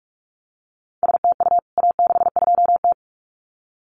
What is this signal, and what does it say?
Morse code, heard as a keyed tone, about 44 wpm, 700 Hz. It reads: STU A62T
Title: CWT Exchange 45wpm